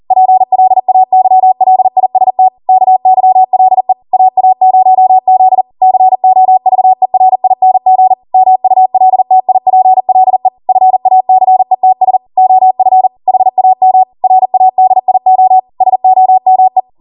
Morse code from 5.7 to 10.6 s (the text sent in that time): COVERING MULTIPLE